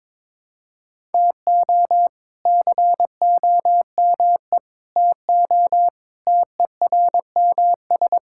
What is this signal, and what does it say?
Morse code, heard as a keyed tone, about 22 wpm, 700 Hz. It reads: TO COME TO TERMS
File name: practice / 22wpm / Sets of 6 Words - 22wpm